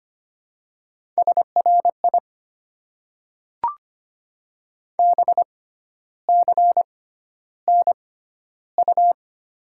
Morse code, heard as a keyed tone, about 25 wpm, spaced as for 12 wpm, 700 Hz. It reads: SRI BCNU
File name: Top 90 QSO Elements - 6x Character Spacing - 25wpm